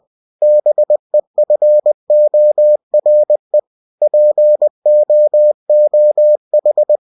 Morse code, heard as a keyed tone, about 20 wpm, 600 Hz. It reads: BEFORE POOH